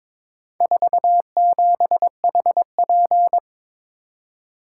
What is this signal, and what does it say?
Morse code, heard as a keyed tone, about 23 wpm, 700 Hz. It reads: 47HP
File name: practice / 22wpm / Four Letter-Number Combinations - 22wpm